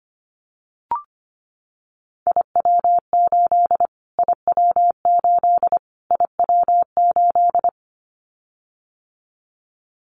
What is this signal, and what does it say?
Morse code, heard as a keyed tone, about 25 wpm, 700 Hz. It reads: IW8 IW8 IW8
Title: Callsign Prefixes for Top 25 DXCC - Repeated 3x - 25wpm